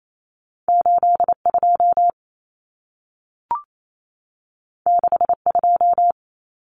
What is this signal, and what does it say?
Morse code, heard as a keyed tone, about 28 wpm, 700 Hz. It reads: 82 62
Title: Common Cut Numbers - 2-Digit 28wpm